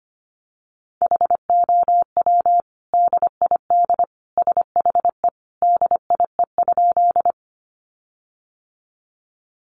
Morse code, heard as a keyed tone, about 25 wpm, 700 Hz. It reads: HOW DID SHE DIE?
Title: Sentences from Top 1300 Words 25wpm